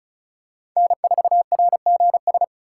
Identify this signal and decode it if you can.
Morse code, heard as a keyed tone, about 36 wpm, 700 Hz. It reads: N4RGS
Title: USA Call Signs - 5 Characters - 8x Character Spacing - 35wpm